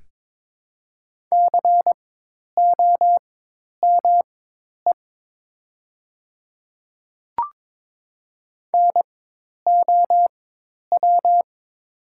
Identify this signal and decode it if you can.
Morse code, heard as a keyed tone, about 22 wpm, spaced as for 10 wpm, 700 Hz. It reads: COME NOW
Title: 4x Character Spacing - Top 100 Words - 22wpm